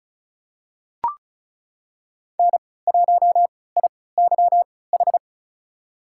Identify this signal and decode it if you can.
Morse code, heard as a keyed tone, about 36 wpm, spaced as for 20 wpm, 700 Hz. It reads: N1IYH
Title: USA Call Signs - 3x Character Spacing - 35wpm